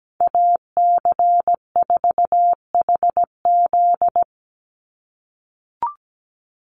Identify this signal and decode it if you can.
Morse code, heard as a keyed tone, about 17 wpm, 700 Hz. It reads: AC4HZ
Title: USA Call Signs 17wpm